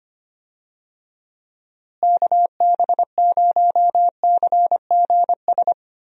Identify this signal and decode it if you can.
Morse code, heard as a keyed tone, about 25 wpm, 700 Hz. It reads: KB0CGS
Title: USA Call Signs - 6 Characters - Repeated 2x - 25wpm